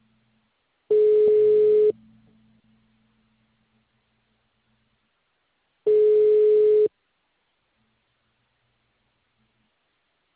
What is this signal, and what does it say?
Ringback tone (Europe, 425 Hz).